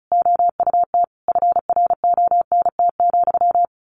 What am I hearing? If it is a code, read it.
Morse code, heard as a keyed tone, about 34 wpm, 700 Hz. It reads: OUT FRONT,
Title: A Sound Of Thunder 35wpm